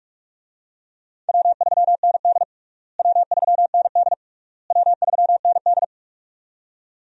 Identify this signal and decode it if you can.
Morse code, heard as a keyed tone, about 43 wpm, 700 Hz. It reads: W3ND W3ND W3ND
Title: USA Call Signs - 4 Characters - Repeated 3x - 45wpm